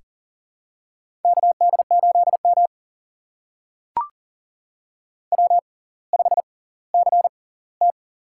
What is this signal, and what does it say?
Morse code, heard as a keyed tone, about 40 wpm, spaced as for 20 wpm, 700 Hz. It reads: KD8M W5CT
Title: USA Call Signs - 4 Characters - 6x Character Spacing - 40wpm